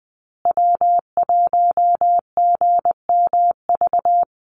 Morse code, heard as a keyed tone, about 20 wpm, 700 Hz. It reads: W1GMV
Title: USA Call Signs Repeated 3x 20wpm